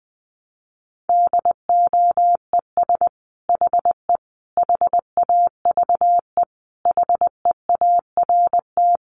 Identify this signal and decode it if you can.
Morse code, heard as a keyed tone, about 19 wpm, 700 Hz. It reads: DOES HE HAVE HEART
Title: Sentences from Top 1200 Words 20wpm